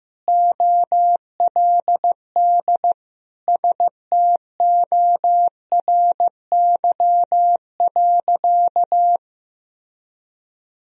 Morse code, heard as a keyed tone, about 15 wpm, 700 Hz. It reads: OLD STORY.
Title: Sentences from Top 1000 Words - Review Part 1 - 15wpm